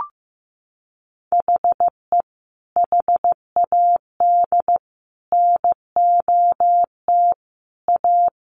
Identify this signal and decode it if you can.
Morse code, heard as a keyed tone, about 15 wpm, 700 Hz. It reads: HE HAD NOT A
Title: Sentences from Top 1100 Words 15wpm